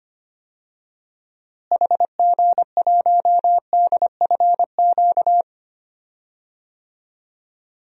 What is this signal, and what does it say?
Morse code, heard as a keyed tone, about 25 wpm, 700 Hz. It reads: HG1DFQ
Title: DXCC-100 Call Signs - 6 Characters - Repeated 3x - 25wpm